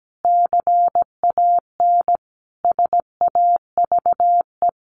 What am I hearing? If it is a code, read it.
Morse code, heard as a keyed tone, about 17 wpm, 700 Hz. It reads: CAN SAVE
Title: Sentences from Top 1100 Words 17wpm